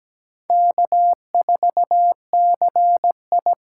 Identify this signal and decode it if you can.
Morse code, heard as a keyed tone, about 17 wpm, 700 Hz. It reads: K4CI
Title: USA Call Signs - 5 Characters - 8x Character Spacing - 17wpm